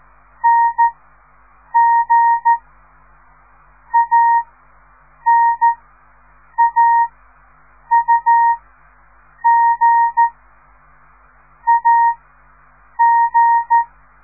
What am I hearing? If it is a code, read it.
Morse code, heard as a keyed tone, about 14 wpm, 930 Hz. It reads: N G A N A U G A G
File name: radiomorningstar / morse / les 4